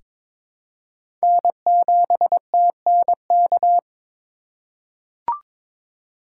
Morse code, heard as a keyed tone, about 22 wpm, 700 Hz. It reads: N7TNK
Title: USA Call Signs - 5 Characters - 3x Character Spacing - 22wpm